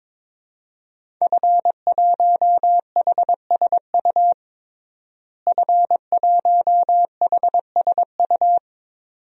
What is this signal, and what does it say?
Morse code, heard as a keyed tone, about 22 wpm, 700 Hz. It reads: F1HSU F1HSU
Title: DXCC-100 Call Signs - 5 Characters - Repeated 2x - 22wpm